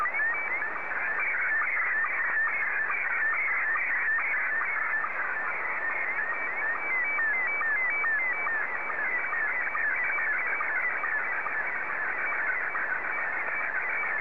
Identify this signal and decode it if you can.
SSTV (slow-scan television), Scottie 1.